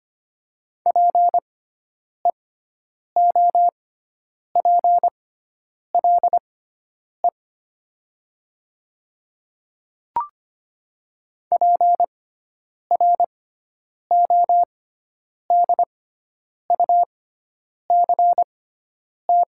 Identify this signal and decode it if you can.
Morse code, heard as a keyed tone, about 25 wpm, spaced as for 9 wpm, 700 Hz. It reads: PEOPLE PRODUCT
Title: 6x Character Spacing - Top 500 Words - 25wpm